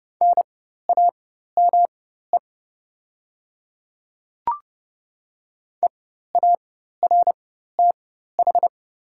Morse code, heard as a keyed tone, about 30 wpm, spaced as for 14 wpm, 700 Hz. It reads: NAME EARTH
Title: 4x Character Spacing - Top 200 Words - 30wpm